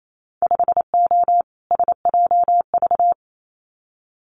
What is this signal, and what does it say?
Morse code, heard as a keyed tone, about 28 wpm, 700 Hz. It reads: 5O SJV
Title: Winter Field Day Exchanges Encore 28wpm